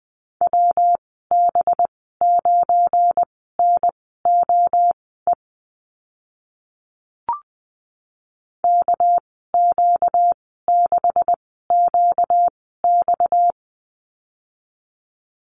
Morse code, heard as a keyed tone, about 20 wpm, spaced as for 14 wpm, 700 Hz. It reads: WB9NOE KQ6QX
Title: 245 Lesson 39g - Letter z - Call Signs 20wpm